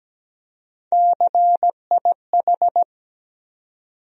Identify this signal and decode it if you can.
Morse code, heard as a keyed tone, about 17 wpm, 700 Hz. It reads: CIH
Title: USA Call Sign Postfixes - 3 Letters 17wpm